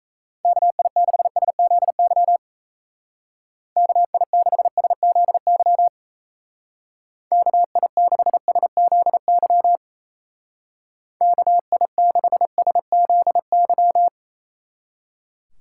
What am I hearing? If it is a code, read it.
Morse code, heard as a keyed tone, about 35 wpm, 700 Hz. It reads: KI6SZY KI6SZY KI6SZY KI6SZY